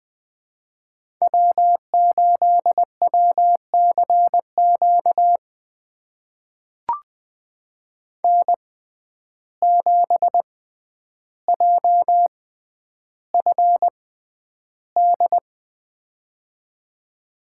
Morse code, heard as a keyed tone, about 20 wpm, spaced as for 10 wpm, 700 Hz. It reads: W8WCQ N7JFD
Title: USA Call Signs - 5 Characters - 6x Character Spacing - 20wpm